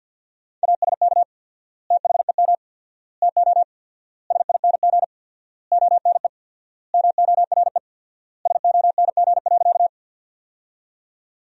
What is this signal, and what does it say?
Morse code, heard as a keyed tone, about 50 wpm, 700 Hz. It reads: ASK THEM TO SING ONE MORE SONG.
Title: 4x Word-Spacing - Sentences from Top 400 Words - 50wpm